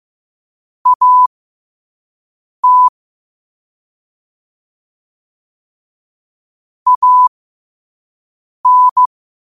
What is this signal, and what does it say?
Morse code, heard as a keyed tone, about 15 wpm, spaced as for 5 wpm, 1000 Hz. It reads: AT AN
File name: qotdfiles / 5wpm